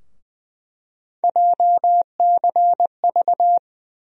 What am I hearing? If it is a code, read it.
Morse code, heard as a keyed tone, about 21 wpm, 700 Hz. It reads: JCV
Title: USA Call Sign Postfixes - 3 Letters 20wpm